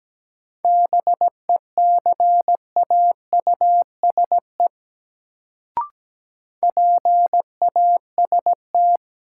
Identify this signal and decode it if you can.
Morse code, heard as a keyed tone, about 17 wpm, 700 Hz. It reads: BECAUSE PAST
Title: Sentences from Top 600 Words - New Word Drill - 17wpm